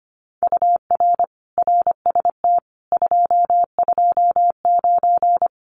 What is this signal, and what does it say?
Morse code, heard as a keyed tone, about 25 wpm, 700 Hz. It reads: UR RST 229
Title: Common QSO Phrases Encore 25wpm